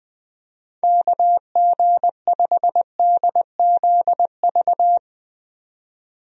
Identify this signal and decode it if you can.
Morse code, heard as a keyed tone, about 20 wpm, 700 Hz. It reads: KG5DZV
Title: USA Call Signs - 6 Characters - 20wpm